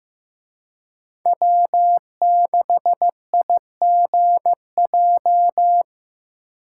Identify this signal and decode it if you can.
Morse code, heard as a keyed tone, about 18 wpm, 700 Hz. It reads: W6IGJ
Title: Speed Racing - USA 1x3 Call Signs 15wpm